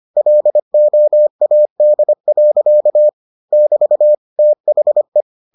Morse code, heard as a keyed tone, about 25 wpm, 600 Hz. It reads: LOAD. <BT> THE